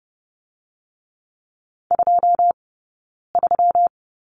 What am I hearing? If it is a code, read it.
Morse code, heard as a keyed tone, about 30 wpm, 700 Hz. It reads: 2 3
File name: Lesson 20d - Number 3 - Two Characters 30wpm